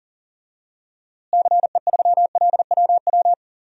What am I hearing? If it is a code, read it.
Morse code, heard as a keyed tone, about 40 wpm, 700 Hz. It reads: CE3LWW
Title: DXCC-100 Call Signs - 6 Characters - Repeated 3x - 40wpm